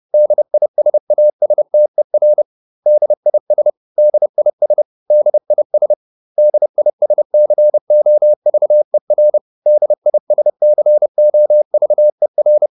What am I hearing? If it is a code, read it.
Morse code, heard as a keyed tone, about 30 wpm, 600 Hz. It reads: DISASTER DIS DIS DIS DISCOVER DISCOVER